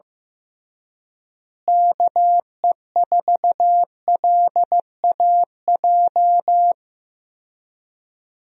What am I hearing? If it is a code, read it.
Morse code, heard as a keyed tone, about 15 wpm, 700 Hz. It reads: KE4LAJ
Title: USA Call Signs - 6 Characters - Repeated 3x - 15wpm